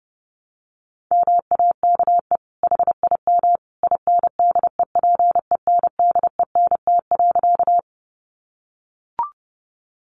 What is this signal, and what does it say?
Morse code, heard as a keyed tone, about 30 wpm, 700 Hz. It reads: MAKE HIM INDEPENDENT.
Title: Sentences from Top 900 Words 30wpm